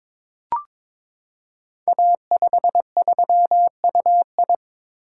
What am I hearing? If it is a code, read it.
Morse code, heard as a keyed tone, about 22 wpm, 700 Hz. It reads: A53UI
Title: Code Groups - 5 Letter:Numbers - 22wpm